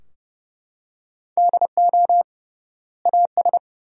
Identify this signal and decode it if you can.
Morse code, heard as a keyed tone, about 30 wpm, 700 Hz. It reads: DO AS
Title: 3x Word-Spacing - Sentences from Top 100 Words - 30wpm